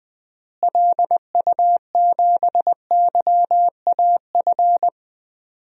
Morse code, heard as a keyed tone, about 20 wpm, 700 Hz. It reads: LU7YAF
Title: DXCC-100 Call Signs - 6 Characters - Repeated 3x - 20wpm